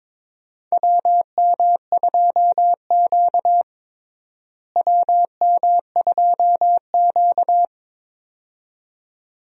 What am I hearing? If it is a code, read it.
Morse code, heard as a keyed tone, about 22 wpm, 700 Hz. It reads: WM2Q WM2Q
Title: USA Call Signs - 4 Characters - Repeated 2x - 22wpm